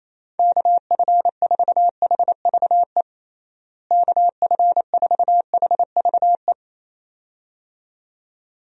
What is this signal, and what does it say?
Morse code, heard as a keyed tone, about 27 wpm, 700 Hz. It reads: KF4HVE KF4HVE
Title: USA Call Signs - 6 Characters - Repeated 2x - 28wpm